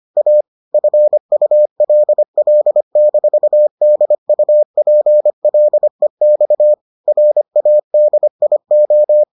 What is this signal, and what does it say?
Morse code, heard as a keyed tone, about 25 wpm, 600 Hz. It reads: A FULL-DUPLEX RADIO